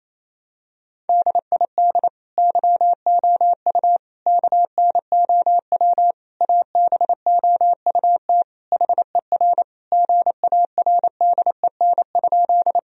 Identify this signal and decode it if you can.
Morse code, heard as a keyed tone, about 28 wpm, 700 Hz. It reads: DID YOU KNOW ABOUT HER GARDEN?
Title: Sentences from Top 1000 Words - Review Part 4 - 28wpm